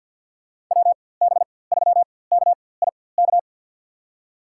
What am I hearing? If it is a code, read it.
Morse code, heard as a keyed tone, about 53 wpm, spaced as for 24 wpm, 700 Hz. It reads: WB3KIK
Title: USA Call Signs - 6 Characters - 4x Character Spacing - 50wpm